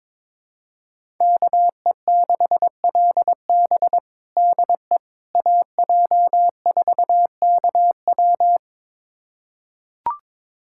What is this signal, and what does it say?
Morse code, heard as a keyed tone, about 22 wpm, 700 Hz. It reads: KE6LB DE AJ4KW
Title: USA Callsign DE Callsign - 5 Characters - 22wpm